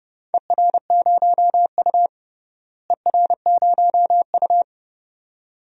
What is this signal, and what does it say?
Morse code, heard as a keyed tone, about 30 wpm, 700 Hz. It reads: ER0U ER0U
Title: DXCC-100 Call Signs - 4 Characters - Repeated 2x - 30wpm